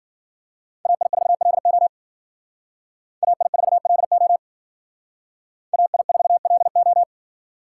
Morse code, heard as a keyed tone, about 52 wpm, 700 Hz. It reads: AI4LO AI4LO AI4LO